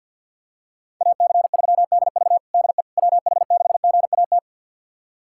Morse code, heard as a keyed tone, about 46 wpm, 700 Hz. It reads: AC3DV DE WH6GAT
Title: USA Callsign DE Callsign 50wpm